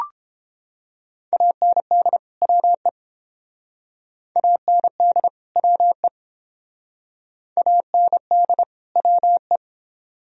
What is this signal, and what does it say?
Morse code, heard as a keyed tone, about 31 wpm, 700 Hz. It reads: AND WE AND WE AND WE